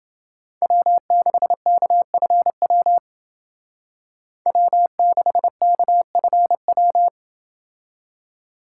Morse code, heard as a keyed tone, about 28 wpm, 700 Hz. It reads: W6KFW W6KFW